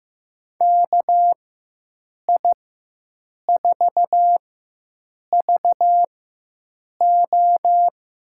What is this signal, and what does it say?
Morse code, heard as a keyed tone, about 15 wpm, spaced as for 7 wpm, 700 Hz. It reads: KI4VO
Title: USA Call Signs - 4x Character Spacing - 15wpm